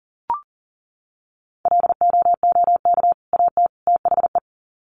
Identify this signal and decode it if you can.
Morse code, heard as a keyed tone, about 40 wpm, 700 Hz. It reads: LOOK AT THE
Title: Sets of 3 Words 40wpm